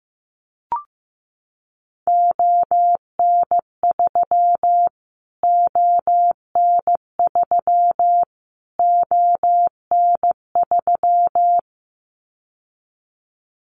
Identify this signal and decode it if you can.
Morse code, heard as a keyed tone, about 15 wpm, 700 Hz. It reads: ON3 ON3 ON3
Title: Callsign Prefixes for Top 25 DXCC - Repeated 3x - 15wpm